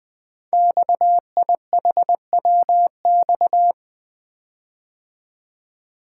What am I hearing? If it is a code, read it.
Morse code, heard as a keyed tone, about 20 wpm, 700 Hz. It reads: XIHWX
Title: Code Groups - 5 Letters - 20wpm